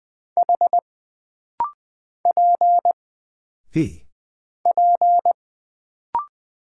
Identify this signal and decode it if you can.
Morse code, heard as a keyed tone, about 20 wpm, 700 Hz. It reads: H P P
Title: Single Character (Letter:Number:Punctuation:Prosign) - Rapid-Fire - 20wpm